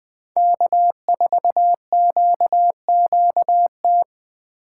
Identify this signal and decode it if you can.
Morse code, heard as a keyed tone, about 20 wpm, 700 Hz. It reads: K4QQT
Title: USA Call Signs - 5 Characters - 4x Character Spacing - 20wpm